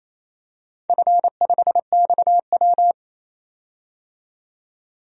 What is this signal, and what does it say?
Morse code, heard as a keyed tone, about 28 wpm, 700 Hz. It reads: F5XW
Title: DXCC-100 Call Signs - 4 Characters - 28wpm